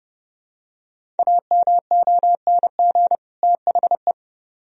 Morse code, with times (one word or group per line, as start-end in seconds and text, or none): 1.2-3.2 AMONG
3.4-4.1 THE